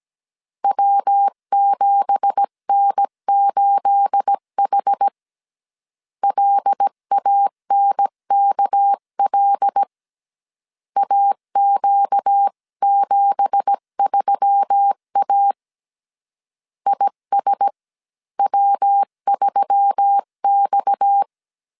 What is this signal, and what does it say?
Morse code, heard as a keyed tone, about 17 wpm, 800 Hz. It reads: W7N8H LANKL AQ73A ISW3X
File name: 1500C S B 16wpm